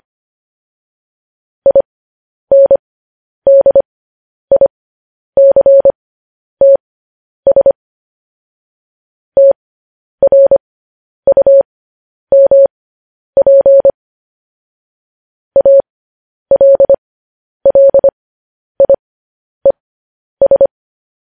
Morse code, heard as a keyed tone, about 25 wpm, spaced as for 10 wpm, 550 Hz. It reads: INDICTS TRUMP ALLIES